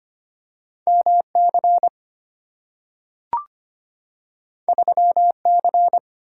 Morse code, heard as a keyed tone, about 25 wpm, 690 Hz. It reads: MC 3C
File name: Two Letter-Number Combinations - 25wpm